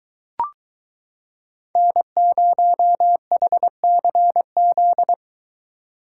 Morse code, heard as a keyed tone, about 23 wpm, 700 Hz. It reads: N0HCZ